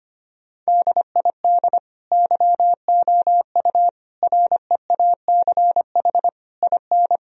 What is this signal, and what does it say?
Morse code, heard as a keyed tone, about 25 wpm, 700 Hz. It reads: DID YOU REACH IN
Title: Sentences from Top 1000 Words - Review Part 3 - 25wpm